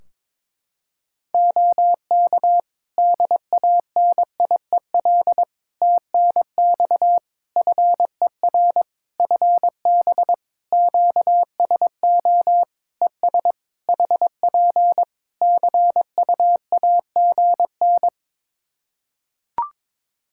Morse code, heard as a keyed tone, about 22 wpm, 700 Hz. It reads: OK DANIEL TNX FER FB QSO ES HP CUAGN
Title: Practice QSO - Exchange 4 - Ending - 22wpm